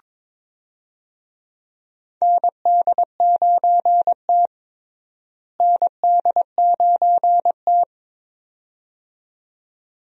Morse code, heard as a keyed tone, about 22 wpm, 700 Hz. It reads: ND9T ND9T
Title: USA Call Signs - 4 Characters - Repeated 2x - 22wpm